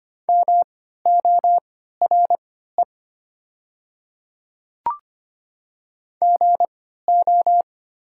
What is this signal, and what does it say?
Morse code, heard as a keyed tone, about 25 wpm, spaced as for 14 wpm, 690 Hz. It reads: MORE GO
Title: 3x Character Spacing - Top 100 Words - 25wpm